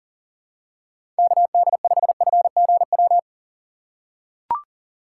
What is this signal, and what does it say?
Morse code, heard as a keyed tone, about 40 wpm, 700 Hz. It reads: KD5FGW
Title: USA Call Signs - 6 Characters - 3x Character Spacing - 40wpm